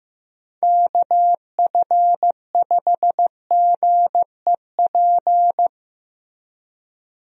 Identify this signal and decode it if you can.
Morse code, heard as a keyed tone, about 16 wpm, 700 Hz. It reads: KF5GEP
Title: USA Call Signs - 6 Characters - Repeated 2x - 15wpm